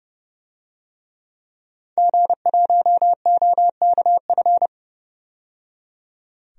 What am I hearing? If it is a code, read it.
Morse code, heard as a keyed tone, about 30 wpm, 700 Hz. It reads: G1OKF
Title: DXCC-100 Call Signs - 5 Characters - 30wpm